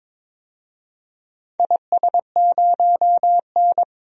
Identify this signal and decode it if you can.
Morse code, heard as a keyed tone, about 21 wpm, 700 Hz. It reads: IS0N
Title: DXCC-100 Call Signs - 4 Characters - Repeated 2x - 22wpm